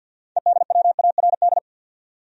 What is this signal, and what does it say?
Morse code, heard as a keyed tone, about 50 wpm, 700 Hz. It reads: EDWARD
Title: USA Common Names - 50wpm